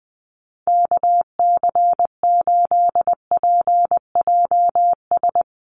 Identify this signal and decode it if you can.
Morse code, heard as a keyed tone, about 20 wpm, 700 Hz. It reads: KC8PJS
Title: USA Call Signs Repeated 3x 20wpm